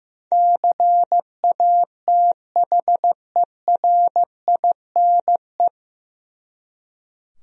Morse code, heard as a keyed tone, about 15 wpm, 700 Hz. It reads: CATHERINE
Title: USA Common Names - 15wpm